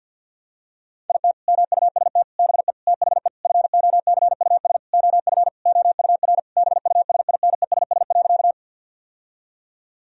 Morse code, heard as a keyed tone, about 50 wpm, 700 Hz. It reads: IT MUST BE THE FOCUS OF OUR BUSINESS.
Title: Sentences from Top 1000 Words - Review Part 4 - 50wpm